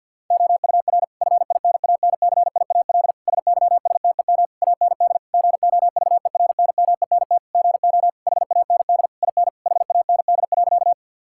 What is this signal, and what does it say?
Morse code, heard as a keyed tone, about 52 wpm, 700 Hz. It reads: OUR FINANCIAL SYSTEM AND GOVERNMENT GO HAND IN HAND.
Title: Sentences from Top 1000 Words - Review Part 5 - 50wpm